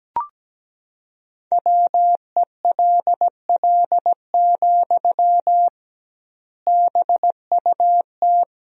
Morse code, heard as a keyed tone, about 17 wpm, 700 Hz. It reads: WELL, BUT
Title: 2x Word-Spacing - Sentences from Top 500 Words - 17wpm